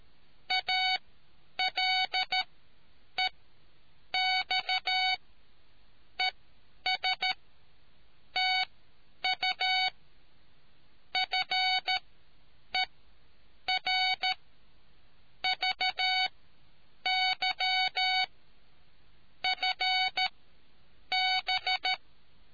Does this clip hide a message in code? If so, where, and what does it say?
Yes: Morse code from 0.5 to 21.9 s, heard as a keyed tone, about 13 wpm, spaced as for 8 wpm, 750 Hz. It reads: ALEX ES TU FER VY FB